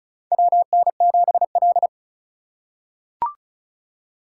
Morse code, heard as a keyed tone, about 36 wpm, 700 Hz. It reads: WN7L
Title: USA Call Signs - 4 Characters - 2x Character Spacing - 35wpm